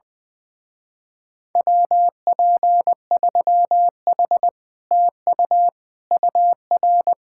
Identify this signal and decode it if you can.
Morse code, heard as a keyed tone, about 20 wpm, 700 Hz. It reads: WP3H TU UR
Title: Speed-Racing Common QSO Phrases - 20wpm